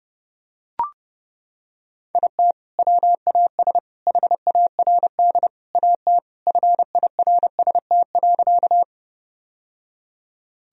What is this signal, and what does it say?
Morse code, heard as a keyed tone, about 32 wpm, 700 Hz. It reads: IT WAS HARD AT FIRST.
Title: Sentences from Top 1000 Words - Review Part 1 - 30wpm